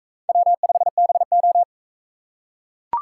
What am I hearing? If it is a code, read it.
Morse code, heard as a keyed tone, about 41 wpm, 700 Hz. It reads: W5BO